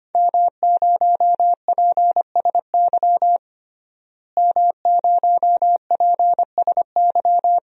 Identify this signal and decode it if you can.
Morse code, heard as a keyed tone, about 25 wpm, 700 Hz. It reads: M0PSY M0PSY
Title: DXCC-100 Call Signs - 5 Characters - Repeated 2x - 25wpm